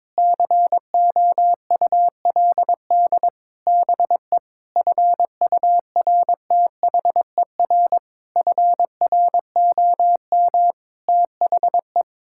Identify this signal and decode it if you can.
Morse code, heard as a keyed tone, about 22 wpm, 700 Hz. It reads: COULD BE FURTHER FROM THE
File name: practice / 22wpm / Sets of 8 Words - 22wpm